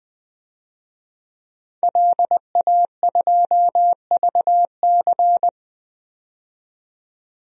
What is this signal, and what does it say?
Morse code, heard as a keyed tone, about 20 wpm, 700 Hz. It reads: LA2VC
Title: DXCC-100 Call Signs - 5 Characters - 20wpm